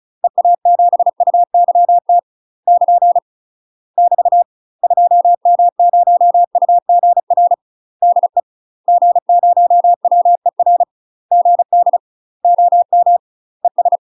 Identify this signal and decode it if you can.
Morse code, heard as a keyed tone, about 36 wpm, 700 Hz. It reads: EA7UYT <KN> <BT> 2M0UGR DE G0WER GD OM ES
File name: UK MASTER QSOs TRAINING/35WPM